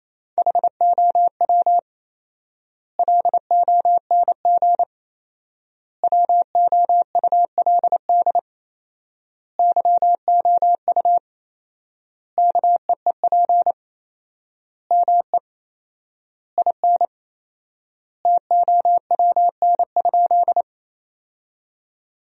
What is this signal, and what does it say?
Morse code, heard as a keyed tone, about 28 wpm, 700 Hz. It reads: HOW LONG WOULD YOU KEEP ME IN TOWN?
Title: 4x Word-Spacing - Sentences from Top 500 Words - 28wpm